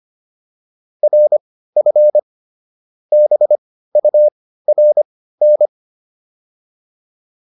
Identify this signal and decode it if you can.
Morse code, heard as a keyed tone, about 25 wpm, spaced as for 15 wpm, 600 Hz. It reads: RF BURN
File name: PR302 15